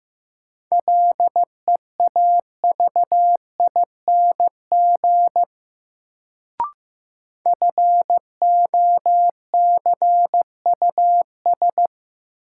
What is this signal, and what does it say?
Morse code, heard as a keyed tone, about 15 wpm, 700 Hz. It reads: LEAVING FOCUS
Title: Sentences from Top 1000 Words - Review Part 4 - New Word Drill - 15wpm